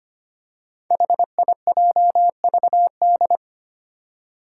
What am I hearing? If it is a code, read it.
Morse code, heard as a keyed tone, about 25 wpm, 700 Hz. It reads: HIJVD
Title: Five Challenging Letter-Number Combinations 25wpm